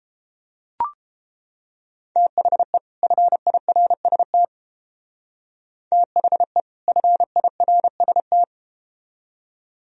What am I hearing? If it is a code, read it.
Morse code, heard as a keyed tone, about 32 wpm, 700 Hz. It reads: THE FIRST THE FIRST